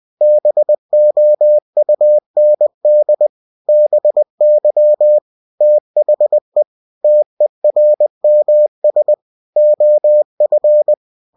Morse code, heard as a keyed tone, about 20 wpm, 600 Hz. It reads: BOUND BY THE TERMS OF
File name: cw-books / winnie-the-pooh / winnie-20wpm-page-0113